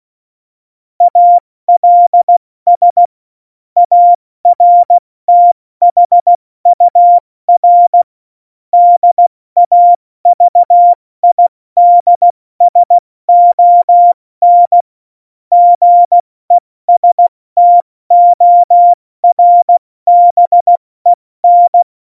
Morse code, heard as a keyed tone, about 16 wpm, 700 Hz. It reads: ALS ARTHUR DAVIDSON GESTORBEN
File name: Morsekurs / Witze / Witz 26. Januar 2021, 70 F80